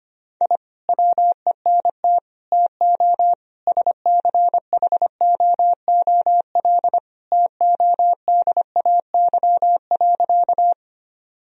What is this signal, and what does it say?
Morse code, heard as a keyed tone, about 25 wpm, 700 Hz. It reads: I WENT TO SCHOOL TODAY.
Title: Sentences from Top 1000 Words - Review Part 1 - 25wpm